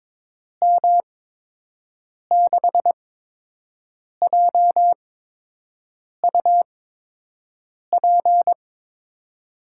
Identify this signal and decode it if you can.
Morse code, heard as a keyed tone, about 22 wpm, spaced as for 6 wpm, 700 Hz. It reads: M6JUP
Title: USA Call Signs - 6 Characters - 8x Character Spacing - 22wpm